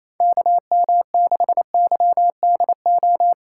Morse code, heard as a keyed tone, about 28 wpm, 700 Hz. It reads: KM6YDO
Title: USA Call Signs - 6 Characters - 4x Character Spacing - 28wpm